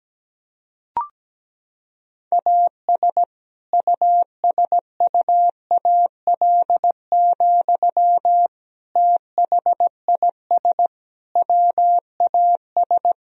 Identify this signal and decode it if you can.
Morse code, heard as a keyed tone, about 17 wpm, 700 Hz. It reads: AS USUAL, THIS WAS
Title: Sentences from Top 1000 Words - Review Part 2 - 17wpm